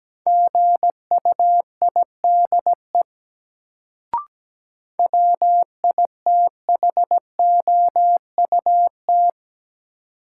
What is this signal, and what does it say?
Morse code, heard as a keyed tone, about 17 wpm, 700 Hz. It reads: GUIDE WITHOUT
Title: Sentences from Top 600 Words - New Word Drill - 17wpm